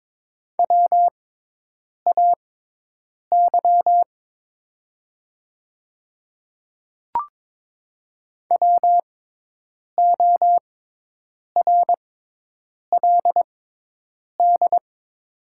Morse code, heard as a keyed tone, about 22 wpm, spaced as for 8 wpm, 700 Hz. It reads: WAY WORLD
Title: 6x Character Spacing - Top 200 Words - 22wpm